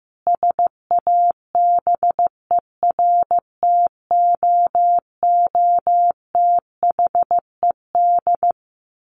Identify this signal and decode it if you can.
Morse code, heard as a keyed tone, about 15 wpm, 700 Hz. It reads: SABERTOOTHED
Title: A Sound Of Thunder 15wpm